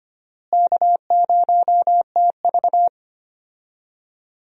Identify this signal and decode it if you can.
Morse code, heard as a keyed tone, about 25 wpm, 700 Hz. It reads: K0TV
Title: USA Call Signs - 4 Characters - Repeated 3x - 25wpm